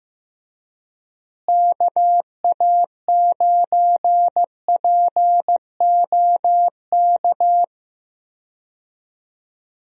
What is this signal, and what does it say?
Morse code, heard as a keyed tone, about 15 wpm, 700 Hz. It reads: KA9POK
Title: USA Call Signs - 6 Characters - Repeated 3x - 15wpm